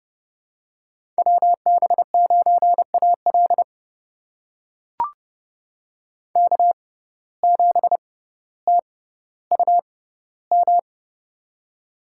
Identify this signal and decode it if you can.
Morse code, heard as a keyed tone, about 30 wpm, spaced as for 15 wpm, 700 Hz. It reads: WB9AL K7TUM
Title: USA Call Signs - 5 Characters - 6x Character Spacing - 30wpm